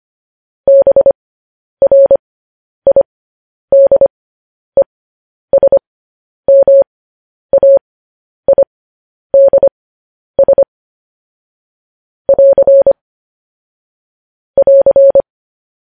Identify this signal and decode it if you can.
Morse code, heard as a keyed tone, about 25 wpm, spaced as for 10 wpm, 550 Hz. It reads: BRIDESMAIDS <AR> <AR>
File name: podcast / News headlines-1x-25x10